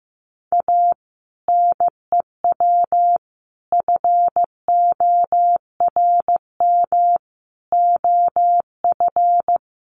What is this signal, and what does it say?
Morse code, heard as a keyed tone, about 15 wpm, 700 Hz. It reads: A NEW FORM OF
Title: Sentences from Top 1300 Words 15wpm